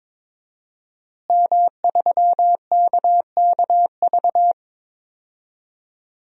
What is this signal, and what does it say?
Morse code, heard as a keyed tone, about 22 wpm, 700 Hz. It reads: M3KKV
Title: DXCC-100 Call Signs - 5 Characters - Repeated 3x - 22wpm